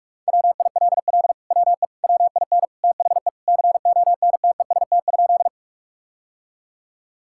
Morse code, heard as a keyed tone, about 44 wpm, 700 Hz. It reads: WILL WE WIN THE CONTEST?
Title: Sentences from Top 1600 Words 45wpm